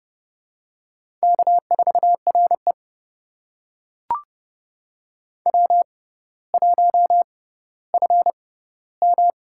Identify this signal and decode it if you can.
Morse code, heard as a keyed tone, about 30 wpm, spaced as for 15 wpm, 700 Hz. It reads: K4RE W1FM
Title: USA Call Signs - 4 Characters - 6x Character Spacing - 30wpm